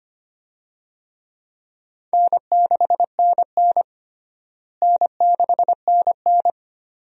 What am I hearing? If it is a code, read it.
Morse code, heard as a keyed tone, about 25 wpm, 700 Hz. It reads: N6NN N6NN
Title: USA Call Signs - 4 Characters - Repeated 2x - 25wpm